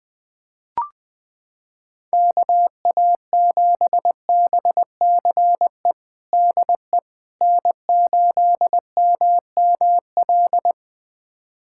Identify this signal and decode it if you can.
Morse code, heard as a keyed tone, about 20 wpm, 700 Hz. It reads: KA7BCE DE N8MML
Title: USA Callsign DE Callsign 20wpm